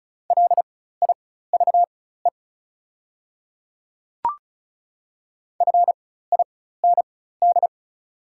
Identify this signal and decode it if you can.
Morse code, heard as a keyed tone, about 33 wpm, spaced as for 16 wpm, 690 Hz. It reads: LIVE FIND
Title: 4x Character Spacing - Top 200 Words - 35wpm